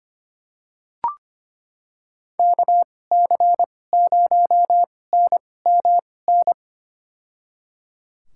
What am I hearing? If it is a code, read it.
Morse code, heard as a keyed tone, about 25 wpm, spaced as for 18 wpm, 700 Hz. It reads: KC0NMN
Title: USA Call Signs - 6 Characters - 2x Character Spacing - 25wpm